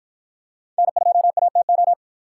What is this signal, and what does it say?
Morse code, heard as a keyed tone, about 51 wpm, 700 Hz. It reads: N2ATO